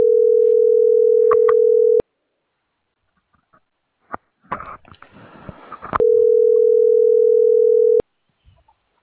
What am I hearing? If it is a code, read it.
Ringback tone (North America), 440 + 480 Hz.